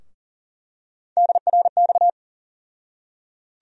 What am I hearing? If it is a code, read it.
Morse code, heard as a keyed tone, about 38 wpm, 700 Hz. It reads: DRX
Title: USA Call Sign Postfixes - 3 Letters 40wpm